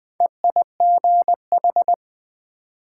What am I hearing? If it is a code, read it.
Morse code, heard as a keyed tone, about 27 wpm, 700 Hz. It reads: EIGH